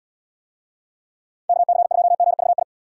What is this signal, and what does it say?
Morse code, heard as a keyed tone, about 76 wpm, 700 Hz. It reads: KC2GVI